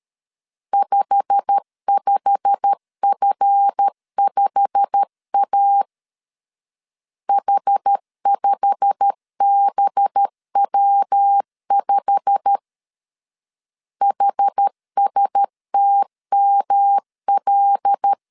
Morse code, heard as a keyed tone, about 13 wpm, 800 Hz. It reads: 55F5A H5BW5 HSTML
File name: LEZIONI / 8 YJ45 / 8 YJ45 12wpm 2151